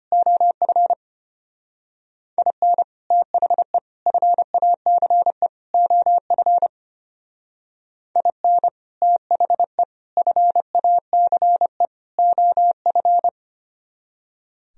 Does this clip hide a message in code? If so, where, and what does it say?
Yes: Morse code from 0.1 to 13.3 s, heard as a keyed tone, about 32 wpm, 700 Hz. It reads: OF IN THE FACE OF IN THE FACE OF